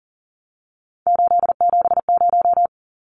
Morse code, heard as a keyed tone, about 40 wpm, 700 Hz. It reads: 870
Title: Numbers - 3-Digit 40wpm